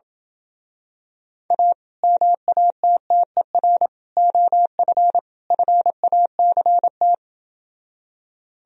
Morse code, heard as a keyed tone, about 28 wpm, 700 Hz. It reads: A MATTER OF FACT